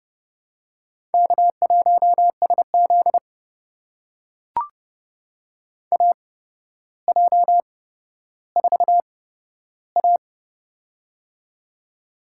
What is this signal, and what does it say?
Morse code, heard as a keyed tone, about 30 wpm, spaced as for 13 wpm, 700 Hz. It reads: K1SZ AJ4A
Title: USA Call Signs - 4 Characters - 8x Character Spacing - 30wpm